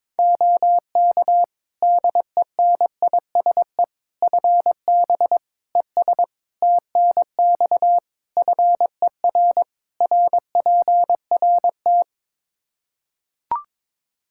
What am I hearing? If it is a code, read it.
Morse code, heard as a keyed tone, about 21 wpm, 700 Hz. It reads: OK DENISE FB ES TNX FER RPRT
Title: K1USN SST QSO Practice 22wpm